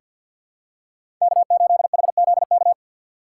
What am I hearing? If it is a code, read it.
Morse code, heard as a keyed tone, about 48 wpm, 700 Hz. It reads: K8HZK
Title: USA Call Signs - 5 Characters - 6x Character Spacing - 50wpm